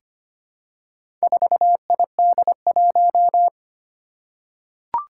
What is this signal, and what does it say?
Morse code, heard as a keyed tone, about 25 wpm, 700 Hz. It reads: 4ID1
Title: Four Challenging Letter-Number Combinations 25wpm